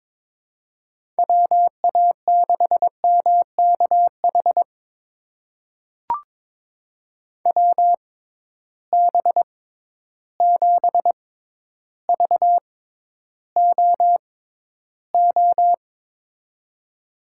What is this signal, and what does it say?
Morse code, heard as a keyed tone, about 21 wpm, spaced as for 11 wpm, 700 Hz. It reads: WA6MKH WB7VOO
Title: USA Call Signs - 6 Characters - 6x Character Spacing - 22wpm